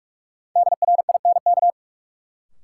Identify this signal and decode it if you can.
Morse code, heard as a keyed tone, about 47 wpm, 700 Hz. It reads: DRINK
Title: Sentences from Top 1500 Words - New Word Drill Alt - 45wpm